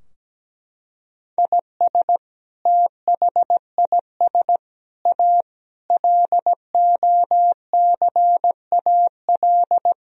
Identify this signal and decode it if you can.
Morse code, heard as a keyed tone, about 17 wpm, 700 Hz. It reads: IS THIS A LOCAL
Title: Sentences from Top 1000 Words - Review Part 3 - 17wpm